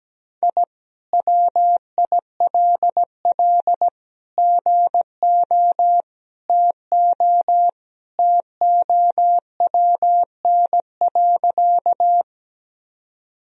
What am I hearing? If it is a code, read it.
Morse code, heard as a keyed tone, about 17 wpm, 700 Hz. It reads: I WILL GO TO TOWN.
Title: Sentences from Top 1000 Words - Review Part 2 - 17wpm